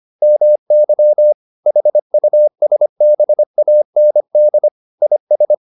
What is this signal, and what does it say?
Morse code, heard as a keyed tone, about 25 wpm, 600 Hz. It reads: MY HUSBAND IS